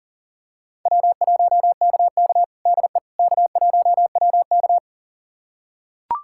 Morse code, heard as a keyed tone, about 37 wpm, 700 Hz. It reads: W1KK DE K1WK
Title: USA Callsign DE Callsign - 4 Characters - 40wpm